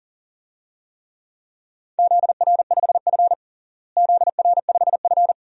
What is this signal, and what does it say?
Morse code, heard as a keyed tone, about 37 wpm, 700 Hz. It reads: ZR5F ZR5F
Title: DXCC-100 Call Signs - 4 Characters - Repeated 2x - 40wpm